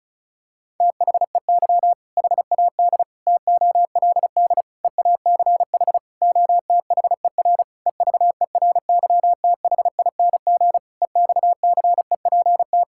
Morse code, heard as a keyed tone, about 34 wpm, 700 Hz. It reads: THEY HAD TOLD EACH OTHER EVERYTHING EXCEPT
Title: Sentences from Top 1600 Words 35wpm